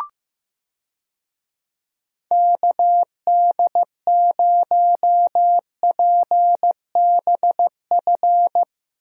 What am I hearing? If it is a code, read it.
Morse code, heard as a keyed tone, about 15 wpm, 700 Hz. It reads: KD0PBF
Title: USA Call Signs - 6 Characters - Repeated 3x - 15wpm